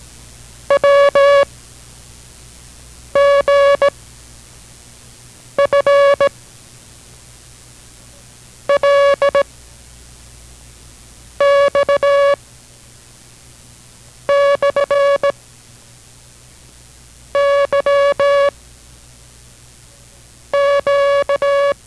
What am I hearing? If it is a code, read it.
Morse code, heard as a keyed tone, about 17 wpm, 560 Hz. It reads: W G F L X / Y Q